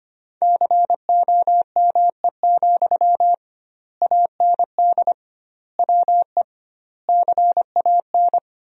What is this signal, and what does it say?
Morse code, heard as a keyed tone, about 25 wpm, 700 Hz. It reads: COME, AND WE CAN
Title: 2x Word-Spacing - Sentences from Top 500 Words - 25wpm